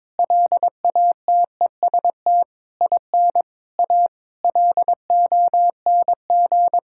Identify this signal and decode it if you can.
Morse code, heard as a keyed tone, about 22 wpm, 700 Hz. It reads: LATEST IN A LONG
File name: practice / 22wpm / Sentences with 8-Word Phrases - 22wpm